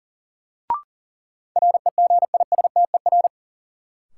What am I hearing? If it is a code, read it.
Morse code, heard as a keyed tone, about 40 wpm, 700 Hz. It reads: REGISTER
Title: Sentences from Top 1200 Words - New Word Drill Alt - 40wpm